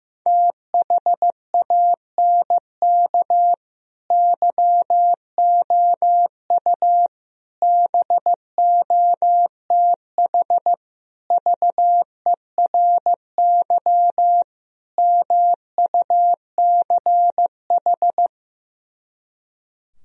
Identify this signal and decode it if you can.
Morse code, heard as a keyed tone, about 15 wpm, 700 Hz. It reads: THANK YOU BOTH VERY MUCH
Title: Speed-Racing - Sets of 5 Words 15wpm